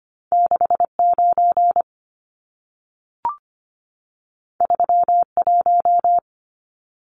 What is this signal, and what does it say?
Morse code, heard as a keyed tone, about 25 wpm, 700 Hz. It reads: 69 31
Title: Numbers - 2-Digit 25wpm